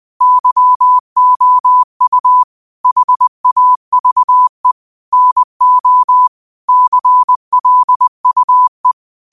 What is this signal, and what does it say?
Morse code, heard as a keyed tone, about 20 wpm, 1000 Hz. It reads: YOU HAVE NO CLUE
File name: qotdfiles / 20wpm